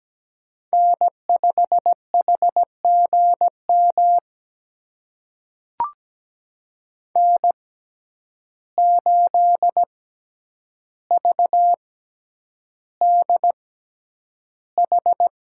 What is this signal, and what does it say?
Morse code, heard as a keyed tone, about 17 wpm, spaced as for 9 wpm, 700 Hz. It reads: N5HGM N8VDH
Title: USA Call Signs - 5 Characters - 6x Character Spacing - 17wpm